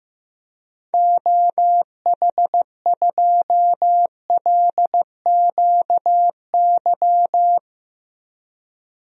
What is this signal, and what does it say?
Morse code, heard as a keyed tone, about 16 wpm, 700 Hz. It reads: OH2LQY
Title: DXCC-100 Call Signs - 6 Characters - 15wpm